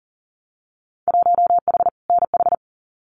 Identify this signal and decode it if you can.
Morse code, heard as a keyed tone, about 41 wpm, 700 Hz. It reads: 1H NH
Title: Winter Field Day Exchanges 40wpm